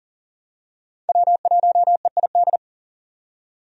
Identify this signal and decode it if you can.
Morse code, heard as a keyed tone, about 40 wpm, 700 Hz. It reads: W1EID
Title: USA Call Signs - 5 Characters - 40wpm